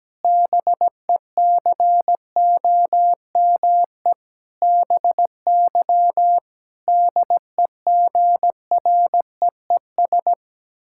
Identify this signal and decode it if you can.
Morse code, heard as a keyed tone, about 17 wpm, 700 Hz. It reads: BECOME BY DEGREES
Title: Pangram Challenge v2 - 17wpm